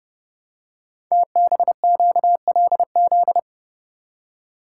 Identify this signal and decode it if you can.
Morse code, heard as a keyed tone, about 30 wpm, 700 Hz. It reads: TBQLZ
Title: Code Groups - 5 Letter:Numbers - 30wpm